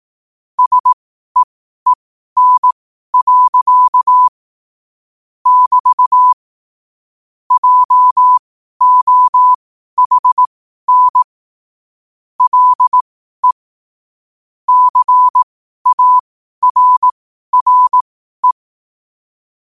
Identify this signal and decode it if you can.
Morse code, heard as a keyed tone, about 18 wpm, spaced as for 12 wpm, 1000 Hz. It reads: SEEN. <BT> JOHN LE CARRE
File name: qotdfiles / 12wpm